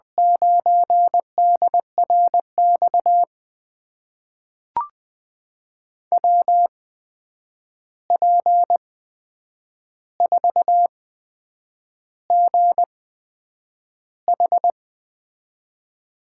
Morse code, heard as a keyed tone, about 20 wpm, spaced as for 8 wpm, 700 Hz. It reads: 9DRX WP4GH
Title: USA Call Signs - 8x Character Spacing - 20wpm